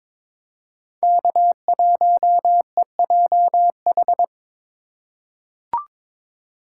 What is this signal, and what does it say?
Morse code, heard as a keyed tone, about 22 wpm, 700 Hz. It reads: K1EJH
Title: USA Call Signs - 5 Characters - 2x Character Spacing - 22wpm